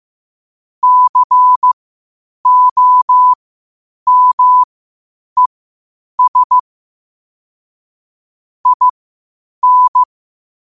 Morse code, heard as a keyed tone, about 15 wpm, spaced as for 8 wpm, 1000 Hz. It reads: COMES IN
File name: qotdfiles / 8wpm